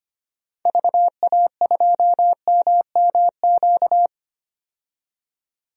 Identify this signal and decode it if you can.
Morse code, heard as a keyed tone, about 25 wpm, 690 Hz. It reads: VA2MMQ
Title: DXCC-100 Call Signs - 6 Characters - 25wpm